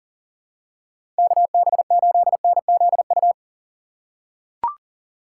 Morse code, heard as a keyed tone, about 37 wpm, 700 Hz. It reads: KB8NZU
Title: USA Call Signs - 6 Characters - 2x Character Spacing - 40wpm